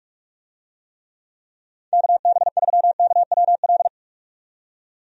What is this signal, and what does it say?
Morse code, heard as a keyed tone, about 47 wpm, 700 Hz. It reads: KB3KWL
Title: USA Call Signs - 6 Characters - 45wpm